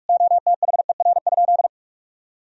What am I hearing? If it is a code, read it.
Morse code, heard as a keyed tone, about 41 wpm, 700 Hz. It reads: OTHER?
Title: Sentences from Top 1000 Words - Review Part 1 - 45wpm